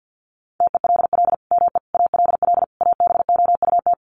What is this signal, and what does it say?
Morse code, heard as a keyed tone, about 50 wpm, 700 Hz. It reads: TELL ME ALL ABOUT
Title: Sentences from Top 1500 Words 50wpm